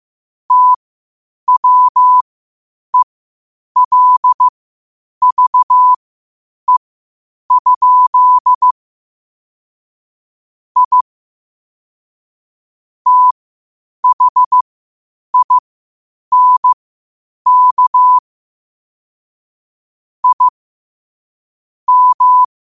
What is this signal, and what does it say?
Morse code, heard as a keyed tone, about 15 wpm, spaced as for 8 wpm, 1000 Hz. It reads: TWELVE? I THINK I M
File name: qotdfiles / 8wpm